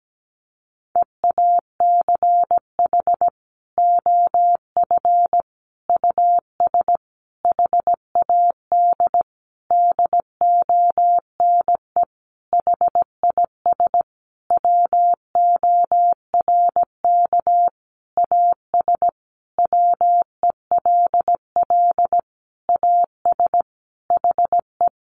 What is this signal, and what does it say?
Morse code, heard as a keyed tone, about 17 wpm, 700 Hz. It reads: EACH OF US HAD DONE HIS WORK AS WELL AS HE
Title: Sentences from Top 500 Words 17wpm